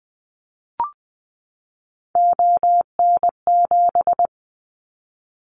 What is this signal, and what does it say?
Morse code, heard as a keyed tone, about 20 wpm, 700 Hz. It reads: ON7
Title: Callsign Prefixes for Top 100 DXCC - 20wpm